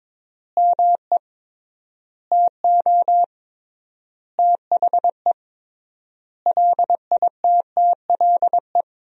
Morse code, heard as a keyed tone, about 22 wpm, 700 Hz. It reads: ME TO THE LITTLE
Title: 3x Word-Spacing - Sentences from Top 200 Words - 22wpm